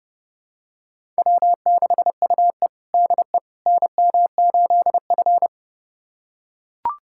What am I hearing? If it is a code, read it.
Morse code, heard as a keyed tone, about 30 wpm, 700 Hz. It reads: W6UE DE NM8F
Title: USA Callsign DE Callsign - 4 Characters - 30wpm